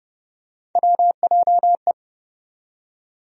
Morse code, heard as a keyed tone, about 32 wpm, 700 Hz. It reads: WJE
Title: USA Call Sign Postfixes - 3 Letters 30wpm